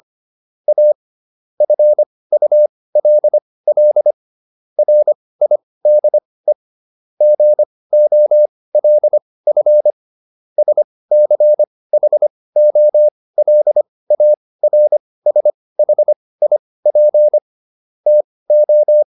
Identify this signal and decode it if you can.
Morse code, heard as a keyed tone, about 25 wpm, spaced as for 18 wpm, 600 Hz. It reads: A FULL RIDE GOLF SCHOLARSHIP TO